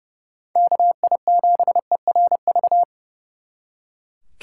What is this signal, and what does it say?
Morse code, heard as a keyed tone, about 32 wpm, 700 Hz. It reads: KI7ERV
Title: Speed Racing - USA 2x3 Call Signs 30wpm